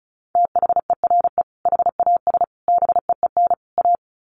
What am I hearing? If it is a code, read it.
Morse code, heard as a keyed tone, about 34 wpm, 700 Hz. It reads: THERE HAS BEEN A
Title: Sets of 4 Words 35wpm